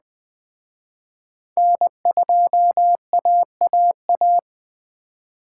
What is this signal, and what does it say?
Morse code, heard as a keyed tone, about 20 wpm, 700 Hz. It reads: N2AAA
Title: Speed Racing - USA 1x3 Call Signs 17wpm